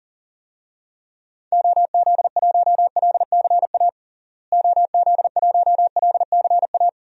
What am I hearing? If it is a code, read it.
Morse code, heard as a keyed tone, about 40 wpm, 700 Hz. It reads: OZ1LCA OZ1LCA
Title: DXCC-100 Call Signs - 6 Characters - Repeated 2x - 40wpm